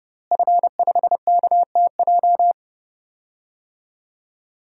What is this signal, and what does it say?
Morse code, heard as a keyed tone, about 32 wpm, 700 Hz. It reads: F5KTJ
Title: DXCC-100 Call Signs - 5 Characters - Repeated 3x - 30wpm